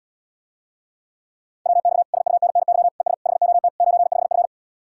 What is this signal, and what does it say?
Morse code, heard as a keyed tone, about 78 wpm, 700 Hz. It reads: WX SUNNY ES HOT 95F